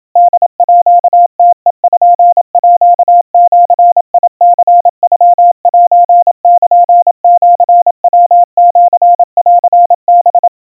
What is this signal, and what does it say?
Morse code (Japanese Wabun code), heard as a keyed tone, about 27 wpm, 700 Hz. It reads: ホームページニノセルシヤシンハ
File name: kawa / Morse sample / kamyu 100 cpm 700 Hz Crispy